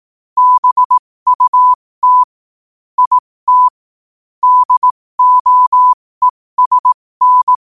Morse code, heard as a keyed tone, about 18 wpm, spaced as for 15 wpm, 1000 Hz. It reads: BUT IT DOESN
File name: qotdfiles / 15wpm